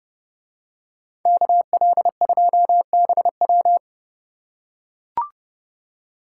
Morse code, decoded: KL2BW

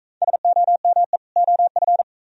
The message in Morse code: SOME OF

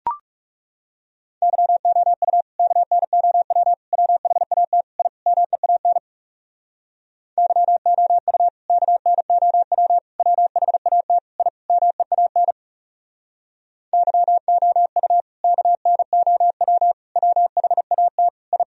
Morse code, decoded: YOU KNOW WHAT I MEAN YOU KNOW WHAT I MEAN YOU KNOW WHAT I

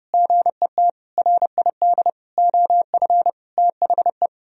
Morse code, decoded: GET RID OF THE

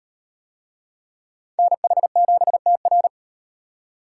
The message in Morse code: NH7TR

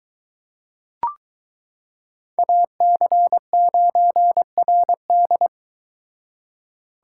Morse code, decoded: AC9RD

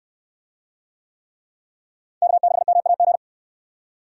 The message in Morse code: K6MAG

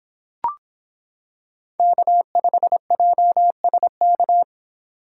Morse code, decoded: K5JSK

